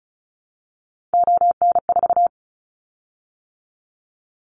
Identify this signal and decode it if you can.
Morse code, heard as a keyed tone, about 34 wpm, 700 Hz. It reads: ON4
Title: Callsign Prefixes for Top 100 DXCC - 35wpm